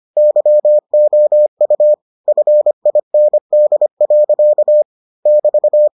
Morse code, heard as a keyed tone, about 25 wpm, 600 Hz. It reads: YOU FIND. <BT>